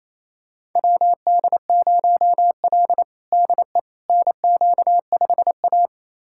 Morse code, heard as a keyed tone, about 28 wpm, 700 Hz. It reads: WD0L DE NQ5A
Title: USA Callsign DE Callsign - 4 Characters - 28wpm